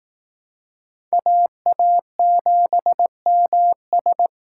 Morse code, heard as a keyed tone, about 18 wpm, 700 Hz. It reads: AA7MS
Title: Speed Racing - USA 2x2 Call Signs 15wpm